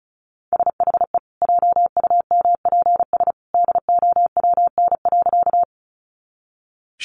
Morse code, heard as a keyed tone, about 34 wpm, 700 Hz. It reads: SHE JUMPS DOWN.
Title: Easy Phrases 35wpm